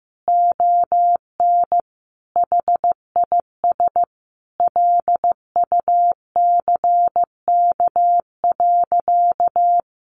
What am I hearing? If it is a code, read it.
Morse code, heard as a keyed tone, about 15 wpm, 700 Hz. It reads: ON HIS LUCK.
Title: Sentences from Top 1500 Words 15wpm